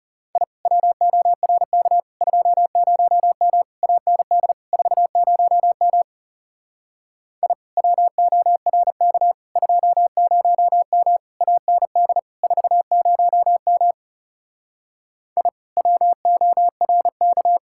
Morse code, heard as a keyed tone, about 38 wpm, 700 Hz. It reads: I WORK 20M AND 40M I WORK 20M AND 40M I WORK